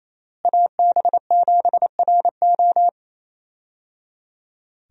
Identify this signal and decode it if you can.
Morse code, heard as a keyed tone, about 28 wpm, 700 Hz. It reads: AB7RO
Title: USA Call Signs - 5 Characters - Repeated 3x - 28wpm